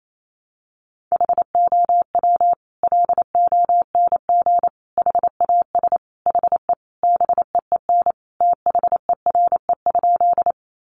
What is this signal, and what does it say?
Morse code, heard as a keyed tone, about 28 wpm, 700 Hz. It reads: HOW LONG HAS HE BEEN THERE?
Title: Sentences from Top 100 Words 28wpm